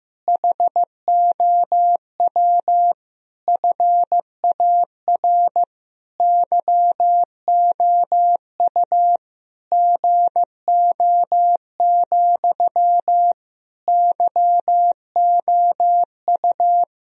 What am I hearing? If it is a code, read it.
Morse code, heard as a keyed tone, about 15 wpm, 700 Hz. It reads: HOW FAR YOU GO, YOU
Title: Sentences from Top 1000 Words - Review Part 1 - 15wpm